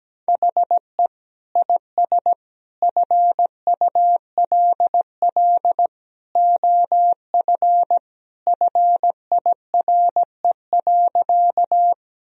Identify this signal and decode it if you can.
Morse code, heard as a keyed tone, about 17 wpm, 700 Hz. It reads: HE IS FULL OF FIRE.
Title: Sentences from Top 1000 Words - Review Part 3 - 17wpm